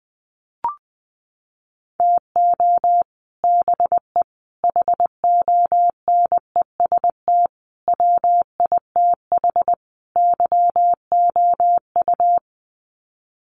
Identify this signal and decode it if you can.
Morse code, heard as a keyed tone, about 20 wpm, 700 Hz. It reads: TO BE HONEST WITH YOU
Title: Sets of 5 Words Encore 20wpm